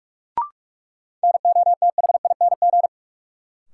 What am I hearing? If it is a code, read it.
Morse code, heard as a keyed tone, about 47 wpm, 700 Hz. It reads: NOTHING
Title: Sentences from Top 500 Words - New Word Drill - 45wpm